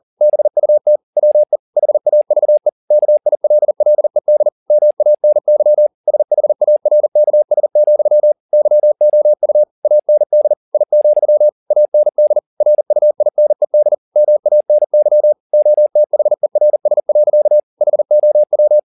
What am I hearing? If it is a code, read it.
Morse code, heard as a keyed tone, about 40 wpm, 600 Hz. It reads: BUT WE HAVE KILLED MANY SHARKS, YOU AND I, AND RUINED MANY OTHERS. HOW
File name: the old man and the sea 2 40wpm